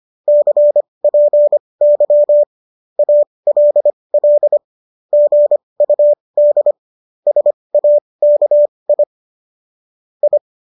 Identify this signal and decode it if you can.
Morse code, heard as a keyed tone, about 25 wpm, spaced as for 20 wpm, 590 Hz. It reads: CPY ALL GUD SAKI I